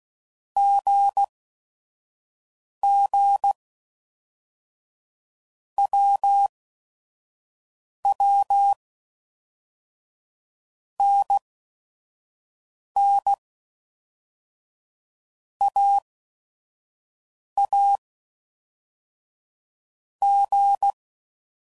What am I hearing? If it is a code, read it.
Morse code, heard as a keyed tone, about 16 wpm, 800 Hz. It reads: G G W W N N A A G
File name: Basic Letter Groups By Blocks